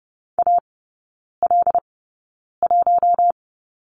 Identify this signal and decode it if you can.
Morse code, heard as a keyed tone, about 30 wpm, spaced as for 9 wpm, 700 Hz. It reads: AL1
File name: Lesson 18e - Letter m - Three Characters 30wpm